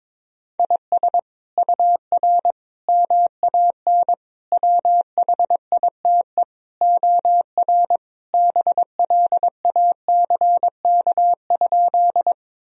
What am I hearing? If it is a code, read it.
Morse code, heard as a keyed tone, about 22 wpm, 700 Hz. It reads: IS UR MAN WHITE OR BLACK?
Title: Sentences from Top 400 Words and Abbreviations 22wpm